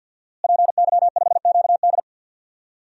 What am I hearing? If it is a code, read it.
Morse code, heard as a keyed tone, about 52 wpm, 700 Hz. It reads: PY5QD
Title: DXCC-100 Call Signs - 5 Characters - Repeated 3x - 50wpm